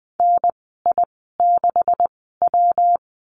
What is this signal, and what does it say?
Morse code, heard as a keyed tone, about 20 wpm, spaced as for 15 wpm, 700 Hz. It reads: NI6W
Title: 231 Lesson 37g - Letter x - Call Signs 20wpm